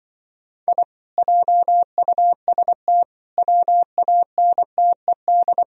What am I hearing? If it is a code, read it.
Morse code, heard as a keyed tone, about 25 wpm, 700 Hz. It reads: I JUST WANTED
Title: Speed-Racing - Sets of 4 Words 20wpm